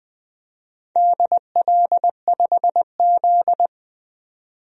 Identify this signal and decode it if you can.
Morse code, heard as a keyed tone, about 20 wpm, 700 Hz. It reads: DL5Z
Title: DXCC-100 Call Signs - 4 Characters - 20wpm